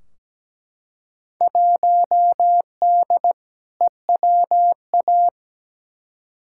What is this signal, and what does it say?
Morse code, heard as a keyed tone, about 17 wpm, 700 Hz. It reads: 1D EWA
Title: Field Day Exchanges - 17wpm